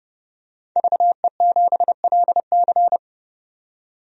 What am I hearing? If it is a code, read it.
Morse code, heard as a keyed tone, about 30 wpm, 690 Hz. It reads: VE7LC
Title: DXCC-100 Call Signs - 5 Characters - 30wpm